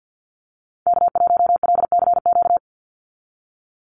Morse code, heard as a keyed tone, about 49 wpm, 700 Hz. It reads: K1LCQ
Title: USA Call Signs 50wpm